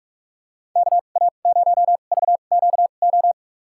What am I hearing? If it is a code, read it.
Morse code, heard as a keyed tone, about 47 wpm, spaced as for 33 wpm, 700 Hz. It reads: KA0VQO
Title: USA Call Signs - 6 Characters - 2x Character Spacing - 45wpm